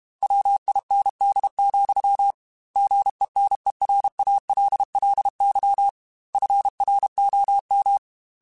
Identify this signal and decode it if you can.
Morse code, heard as a keyed tone, about 32 wpm, 800 Hz. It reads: WIND, GENERALLY FROM
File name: Weather Sets / Weather Sets-mp3s / Weather 4 17 @30